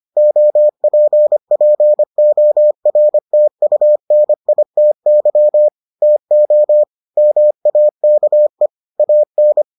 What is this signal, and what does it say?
Morse code, heard as a keyed tone, about 25 wpm, 600 Hz. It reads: OPPORTUNITY TO MAKE AN